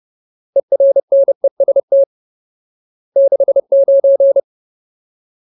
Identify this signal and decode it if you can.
Morse code, heard as a keyed tone, about 30 wpm, 550 Hz. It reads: ERNEST 69
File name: Ultimate Word List / 30WPM Ultimate